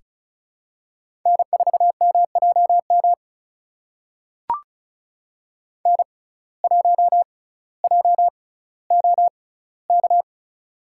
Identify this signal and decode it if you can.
Morse code, heard as a keyed tone, about 34 wpm, spaced as for 18 wpm, 700 Hz. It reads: N4MJM N1JOK
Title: USA Call Signs - 5 Characters - 6x Character Spacing - 35wpm